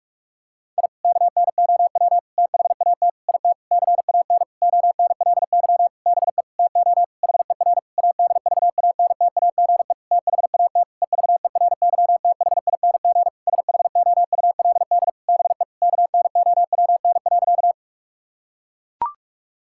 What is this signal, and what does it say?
Morse code, heard as a keyed tone, about 42 wpm, 700 Hz. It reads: I KNOW THAT IT CAN ONLY BE TO HER ADVANTAGE THAT EVERYTHING SHOULD BE KNOWN.
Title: Sentences from Top 1000 Words - Review Part 6 - 45wpm